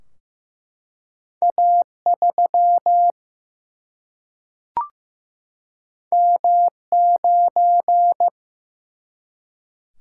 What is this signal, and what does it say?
Morse code, heard as a keyed tone, about 15 wpm, 700 Hz. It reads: A 3 M 9
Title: Single-Letter Single-Number 15wpm